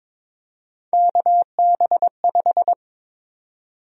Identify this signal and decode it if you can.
Morse code, heard as a keyed tone, about 22 wpm, 700 Hz. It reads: KB5
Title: USA Call Sign Prefixes - 22wpm